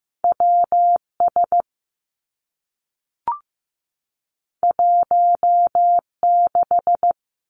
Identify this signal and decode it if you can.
Morse code, heard as a keyed tone, about 15 wpm, 700 Hz. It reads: W S 1 6
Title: Two Challenging Letter-Number Combinations 15wpm